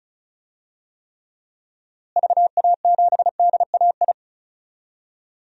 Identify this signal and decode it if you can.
Morse code, heard as a keyed tone, about 36 wpm, 700 Hz. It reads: VA7DAI
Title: DXCC-100 Call Signs - 6 Characters - Repeated 3x - 35wpm